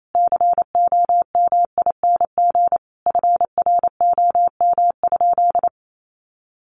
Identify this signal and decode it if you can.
Morse code, heard as a keyed tone, about 28 wpm, 700 Hz. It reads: COMING FROM?
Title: Sentences from Top 1400 Words 28wpm